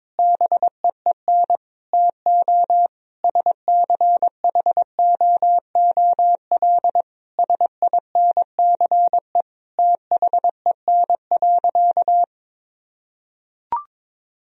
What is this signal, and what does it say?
Morse code, heard as a keyed tone, about 22 wpm, 700 Hz. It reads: BEEN TO SCHOOL SINCE THEN.
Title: Sentences from Top 1000 Words - Review Part 1 - 22wpm